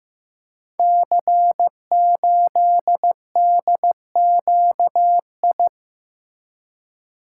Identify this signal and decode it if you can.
Morse code, heard as a keyed tone, about 15 wpm, 700 Hz. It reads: C8DQI
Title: Code Groups - 5 Letter:Numbers - 15wpm